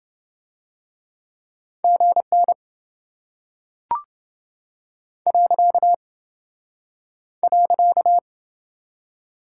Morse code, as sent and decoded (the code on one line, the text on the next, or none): --. -. / .-.-.- / .-.-.-
GN .